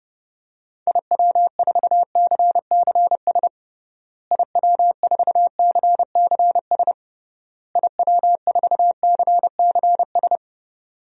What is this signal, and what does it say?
Morse code, heard as a keyed tone, about 28 wpm, 700 Hz. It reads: IW4CCS IW4CCS IW4CCS
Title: DXCC-100 Call Signs - 6 Characters - Repeated 3x - 30wpm